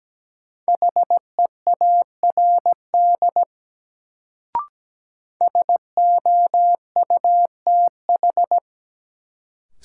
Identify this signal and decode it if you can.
Morse code, heard as a keyed tone, about 17 wpm, 700 Hz. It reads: HEARD SOUTH
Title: Sentences from Top 400 Words - New Word Drill - 17wpm